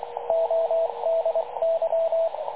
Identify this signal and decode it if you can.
Morse code, heard as a keyed tone, about 22 wpm, 660 Hz, with radio noise. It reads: ODY
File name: dlazaawansowanych / nagraniazpasma / websdr recording 3567.0kHz